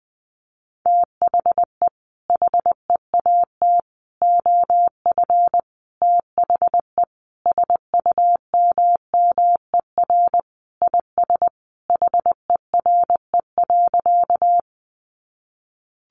Morse code, heard as a keyed tone, about 20 wpm, 700 Hz. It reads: THE HEAT OF THE SUMMER IS HERE.
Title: Sentences from Top 1100 Words 20wpm